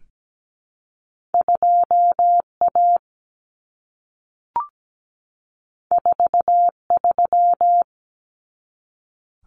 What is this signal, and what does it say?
Morse code, heard as a keyed tone, about 17 wpm, 700 Hz. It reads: 2A 43
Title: Common Cut Numbers - 2-Digit 17wpm